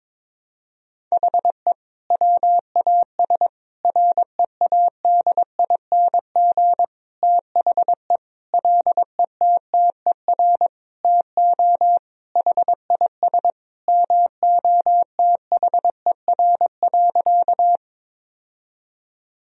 Morse code, heard as a keyed tone, about 22 wpm, 700 Hz. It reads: HE WAS READING THE LETTER TO HIS MOTHER.
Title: Sentences from Top 1000 Words - Review Part 3 - 22wpm